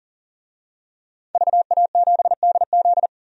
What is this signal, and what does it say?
Morse code, heard as a keyed tone, about 40 wpm, 700 Hz. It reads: VA7DZ
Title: DXCC-100 Call Signs - 5 Characters - 40wpm